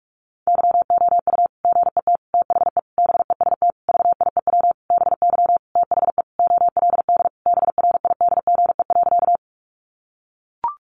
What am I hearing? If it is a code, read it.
Morse code, heard as a keyed tone, about 44 wpm, 700 Hz. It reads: YOU GET THE BEST VIEW BY THE OLD BRIDGE.
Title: Sentences from Top 1200 Words 45wpm